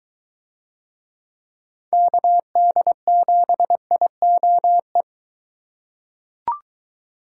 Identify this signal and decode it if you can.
Morse code, heard as a keyed tone, about 22 wpm, 700 Hz. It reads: KD7IOE